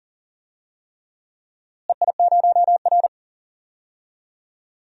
Morse code, heard as a keyed tone, about 40 wpm, 700 Hz. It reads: EI0R
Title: DXCC-100 Call Signs - 4 Characters - Repeated 3x - 40wpm